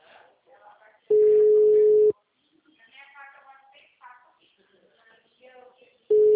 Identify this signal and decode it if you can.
Ringback tone (Europe, 425 Hz), with speech.